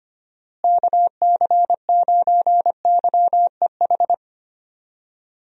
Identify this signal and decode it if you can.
Morse code, heard as a keyed tone, about 25 wpm, 700 Hz. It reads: KC9YEH
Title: USA Call Signs - 6 Characters - Repeated 2x - 25wpm